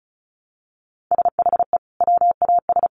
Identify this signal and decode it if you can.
Morse code, heard as a keyed tone, about 34 wpm, 700 Hz. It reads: SHE WAS
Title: Sentences from Top 1500 Words 35wpm